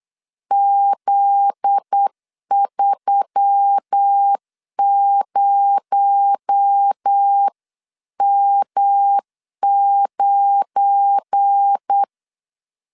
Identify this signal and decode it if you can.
Morse code, heard as a keyed tone, about 8 wpm, 800 Hz. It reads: Z30M9
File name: CW / DET / 1 9 1500C S B 8wpm 5624